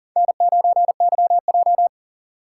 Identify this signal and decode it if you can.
Morse code, heard as a keyed tone, about 40 wpm, 700 Hz. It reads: N9YJ
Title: USA Call Signs - 4 Characters - 4x Character Spacing - 40wpm